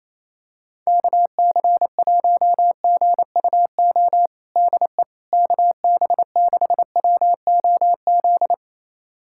Morse code, heard as a keyed tone, about 28 wpm, 700 Hz. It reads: KC1GUO DE KB6WOZ
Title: USA Callsign DE Callsign 28wpm